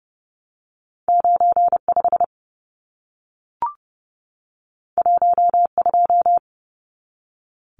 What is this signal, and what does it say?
Morse code, heard as a keyed tone, about 30 wpm, 700 Hz. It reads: 95 12
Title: Numbers - 2-Digit 30wpm